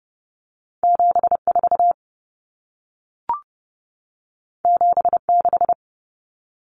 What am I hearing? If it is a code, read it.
Morse code, heard as a keyed tone, about 30 wpm, 700 Hz. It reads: 74 76
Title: Common Cut Numbers - 2-Digit 30wpm